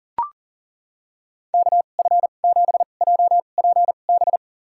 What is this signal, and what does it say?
Morse code, heard as a keyed tone, about 40 wpm, spaced as for 29 wpm, 700 Hz. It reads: KF7JPB
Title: USA Call Signs - 6 Characters - 2x Character Spacing - 40wpm